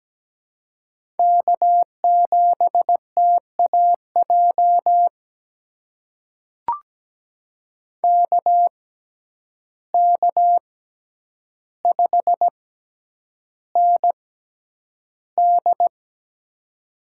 Morse code, heard as a keyed tone, about 17 wpm, spaced as for 9 wpm, 700 Hz. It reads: K7TAJ KK5ND
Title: USA Call Signs - 5 Characters - 6x Character Spacing - 17wpm